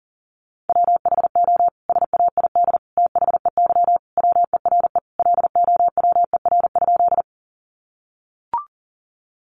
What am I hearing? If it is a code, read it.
Morse code, heard as a keyed tone, about 40 wpm, 700 Hz. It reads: WHO SAID THEY WERE LOWER?
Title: Sentences from Top 700 Words 40wpm